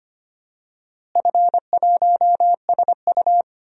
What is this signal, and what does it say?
Morse code, heard as a keyed tone, about 25 wpm, 700 Hz. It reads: F1SU
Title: DXCC-100 Call Signs - 4 Characters - Repeated 2x - 25wpm